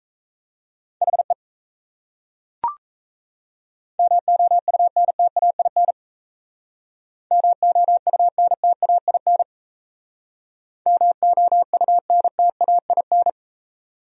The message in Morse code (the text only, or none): HE MOUNTAIN MOUNTAIN MOUNTAIN